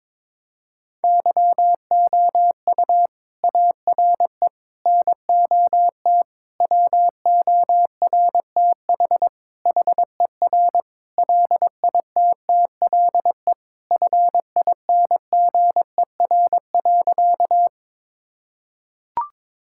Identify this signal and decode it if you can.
Morse code, heard as a keyed tone, about 22 wpm, 700 Hz. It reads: YOU ARE NOT WORTH HER LITTLE FINGER.
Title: Sentences from Top 1600 Words 22wpm